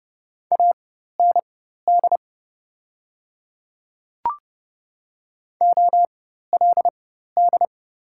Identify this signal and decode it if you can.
Morse code, heard as a keyed tone, about 30 wpm, spaced as for 14 wpm, 700 Hz. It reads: AND OLD
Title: 4x Character Spacing - Top 200 Words - 30wpm